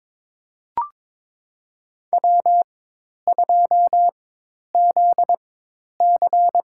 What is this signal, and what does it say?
Morse code, heard as a keyed tone, about 22 wpm, spaced as for 10 wpm, 700 Hz. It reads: W2ZC
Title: USA Call Signs - 4 Characters - 4x Character Spacing - 22wpm